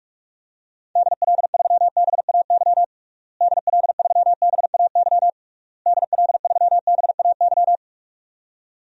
Morse code, heard as a keyed tone, about 41 wpm, 700 Hz. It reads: DL3BAY DL3BAY DL3BAY
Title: DXCC-100 Call Signs - 6 Characters - Repeated 3x - 45wpm